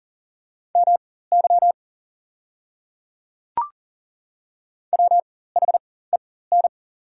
Morse code, heard as a keyed tone, about 40 wpm, spaced as for 19 wpm, 690 Hz. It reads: MY WHEN
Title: 4x Character Spacing - Top 200 Words - 40wpm